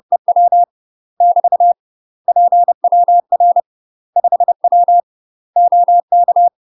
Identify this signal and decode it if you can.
Morse code, heard as a keyed tone, about 30 wpm, 700 Hz. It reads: EW <BT> PWR 5W OK
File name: UK MASTER QSOs TRAINING / 30WPM / UK MASTER QSOs 0045